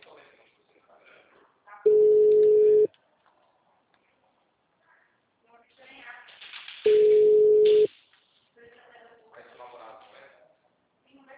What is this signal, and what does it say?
Ringback tone (Europe, 425 Hz), with speech.
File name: out- -2006- - - .43523